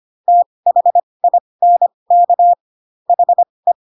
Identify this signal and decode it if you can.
Morse code, heard as a keyed tone, about 25 wpm, spaced as for 20 wpm, 700 Hz. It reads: THINK HE
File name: Pinocchio / Pinocchio ch4 15 25 ik1hge